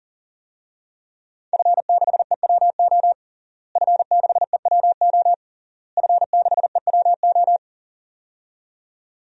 Morse code, heard as a keyed tone, about 40 wpm, 700 Hz. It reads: F6EWO F6EWO F6EWO
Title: DXCC-100 Call Signs - 5 Characters - Repeated 3x - 40wpm